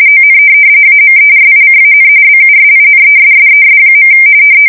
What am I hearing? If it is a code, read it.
RTTY, about 45.45 baud. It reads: ON AMERICAN BUSINESSES.
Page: 170 Shift 2125/2295